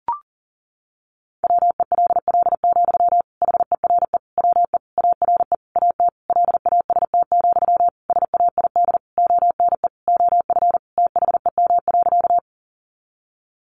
Morse code, heard as a keyed tone, about 40 wpm, 700 Hz. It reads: WELL, HERE WE ARE AT LAST, SAID ONE OF THEM.
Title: Sentences from Top 300 Words 40wpm